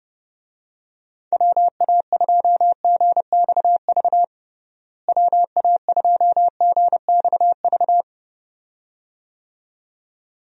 Morse code, heard as a keyed tone, about 32 wpm, 700 Hz. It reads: WA2GXV WA2GXV
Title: USA Call Signs - 6 Characters - Repeated 2x - 30wpm